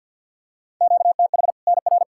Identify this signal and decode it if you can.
Morse code, heard as a keyed tone, about 52 wpm, 700 Hz. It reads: QTH NR
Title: Practice QSO - Exchange 1 - Beginning - 50wpm